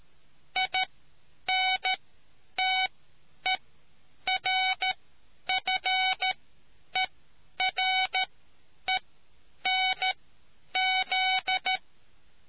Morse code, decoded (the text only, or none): INTERFERENZ